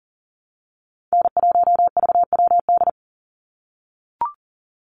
Morse code, heard as a keyed tone, about 40 wpm, 700 Hz. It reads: N1VWD
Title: NA SOTA Chaser Callsigns 40wpm